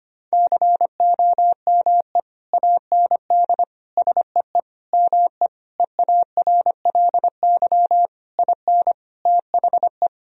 Morse code, heard as a keyed tone, about 25 wpm, 700 Hz. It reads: COME AND SEE ME EARLY IN THE
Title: Sentences from Top 1000 Words - Review Part 2 - 25wpm